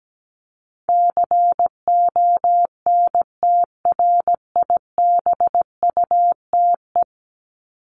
Morse code, heard as a keyed tone, about 17 wpm, 700 Hz. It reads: CONTRIBUTE
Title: Top 1251-1500 Words 17wpm